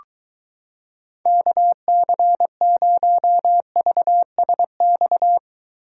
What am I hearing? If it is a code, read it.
Morse code, heard as a keyed tone, about 23 wpm, 700 Hz. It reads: KC0VSX